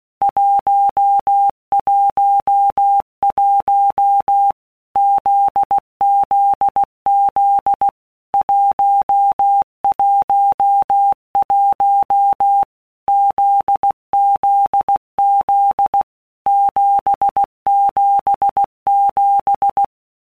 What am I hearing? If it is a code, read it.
Morse code, heard as a keyed tone, about 16 wpm, 800 Hz. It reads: 111 ZZZ 111 ZZZ 777
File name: Basic Letter Groups By Blocks / Groups@15 / JZ17@15